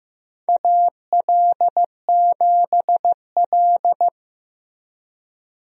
Morse code, heard as a keyed tone, about 15 wpm, 700 Hz. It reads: AL7L
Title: USA Call Signs - 4 Characters - Repeated 3x - 15wpm